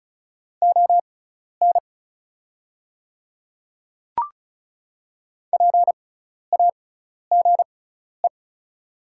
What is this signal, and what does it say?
Morse code, heard as a keyed tone, about 34 wpm, spaced as for 12 wpm, 700 Hz. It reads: ON PAGE
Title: 6x Character Spacing - Top 300 Words - 35wpm